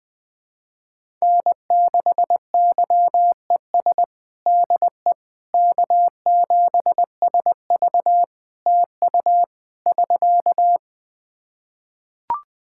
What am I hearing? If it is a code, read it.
Morse code, heard as a keyed tone, about 20 wpm, 700 Hz. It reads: N6YES DE K7SV TU <SK>
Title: K1USN SST QSO Practice 20wpm